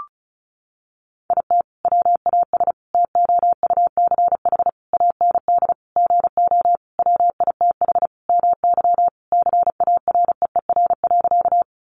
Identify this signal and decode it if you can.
Morse code, heard as a keyed tone, about 34 wpm, 700 Hz. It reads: IT WAS TOUCH AND GO WITH MY CAREER.
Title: Sentences from Top 1200 Words 35wpm